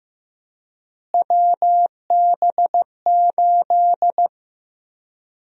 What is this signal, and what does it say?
Morse code, heard as a keyed tone, about 15 wpm, 700 Hz. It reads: WB8
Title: USA Call Sign Prefixes - Repeated 2x - 15wpm